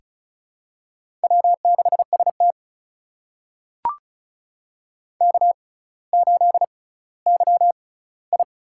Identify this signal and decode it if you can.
Morse code, heard as a keyed tone, about 36 wpm, spaced as for 18 wpm, 700 Hz. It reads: W6ST K8YI
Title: USA Call Signs - 6x Character Spacing - 35wpm